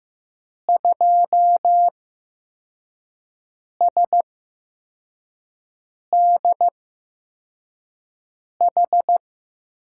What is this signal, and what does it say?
Morse code, heard as a keyed tone, about 15 wpm, spaced as for 4 wpm, 700 Hz. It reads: 2SDH
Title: USA Call Signs - 6 Characters - 8x Character Spacing - 15wpm